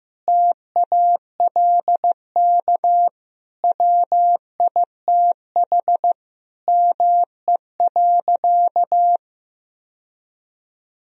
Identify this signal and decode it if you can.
Morse code, heard as a keyed tone, about 16 wpm, 700 Hz. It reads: TALK WITH ME.
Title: Sentences from Top 1000 Words - Review Part 3 - 15wpm